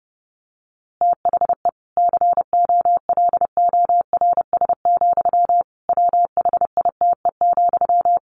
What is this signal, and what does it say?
Morse code, heard as a keyed tone, about 30 wpm, 700 Hz. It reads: THE COLORS, WHITE,
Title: A Sound Of Thunder 30wpm